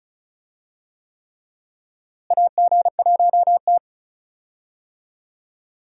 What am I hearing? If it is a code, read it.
Morse code, heard as a keyed tone, about 36 wpm, 700 Hz. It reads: AG1T
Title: USA Call Signs - 4 Characters - Repeated 3x - 35wpm